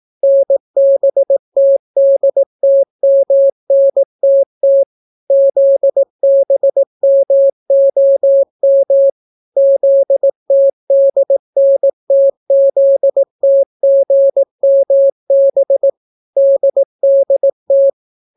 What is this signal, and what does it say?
Morse code, heard as a keyed tone, about 18 wpm, 550 Hz. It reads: NBTDTMNTT ZBMOM ZTDNTZTGMB DDT